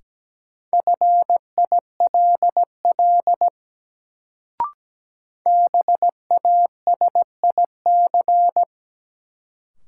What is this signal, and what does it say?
Morse code, heard as a keyed tone, about 17 wpm, 700 Hz. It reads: FILL BASIC
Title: Sentences from Top 1000 Words - Review Part 3 - New Word Drill - 17wpm